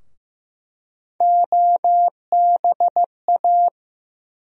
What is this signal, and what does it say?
Morse code, heard as a keyed tone, about 15 wpm, 700 Hz. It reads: OBA
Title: USA Call Sign Postfixes - 3 Letters 15wpm